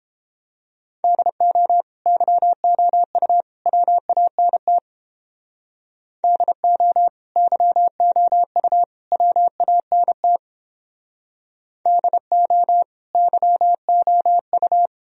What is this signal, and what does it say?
Morse code, heard as a keyed tone, about 29 wpm, 700 Hz. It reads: DO YOU WANT DO YOU WANT DO YOU